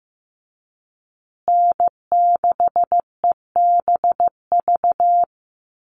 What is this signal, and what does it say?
Morse code, heard as a keyed tone, about 15 wpm, 700 Hz. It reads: N6EBV
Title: USA Call Signs Repeated 3x 15wpm